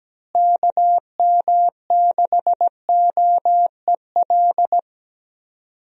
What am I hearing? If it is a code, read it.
Morse code, heard as a keyed tone, about 17 wpm, 700 Hz. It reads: KM6OEL
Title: USA Call Signs - 6 Characters - Repeated 3x - 17wpm